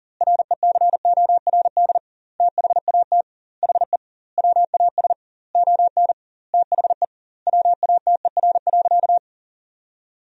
Morse code, heard as a keyed tone, about 40 wpm, 700 Hz. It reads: RECORD THAT HE WAS ON THE WATER.
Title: 2x Word-Spacing - Sentences from Top 500 Words - 40wpm